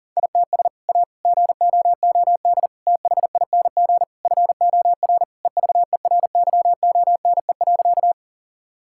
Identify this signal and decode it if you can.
Morse code, heard as a keyed tone, about 40 wpm, 700 Hz. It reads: ITS A GOOD THING FOR EVERYONE.
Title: Sentences from Top 1000 Words - Review Part 5 - 40wpm